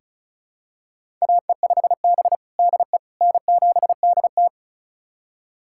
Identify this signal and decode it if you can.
Morse code, heard as a keyed tone, about 36 wpm, 700 Hz. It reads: AE5B DE N7DT
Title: USA Callsign DE Callsign - 4 Characters - 35wpm